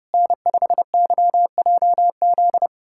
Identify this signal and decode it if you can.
Morse code, heard as a keyed tone, about 32 wpm, 700 Hz. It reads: N5YJZ
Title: USA Callsign DE Callsign - 5 Characters - 30wpm